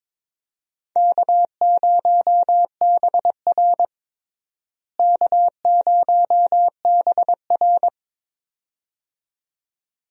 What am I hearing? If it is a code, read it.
Morse code, heard as a keyed tone, about 21 wpm, 700 Hz. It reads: K0BR K0BR
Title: USA Call Signs - 4 Characters - Repeated 2x - 22wpm